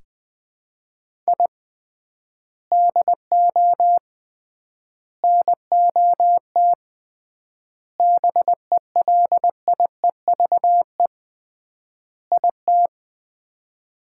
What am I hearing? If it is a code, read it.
Morse code, heard as a keyed tone, about 20 wpm, 700 Hz. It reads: I DO NOT BELIEVE IT
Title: 3x Word-Spacing - Sets of 5 Words - 20wpm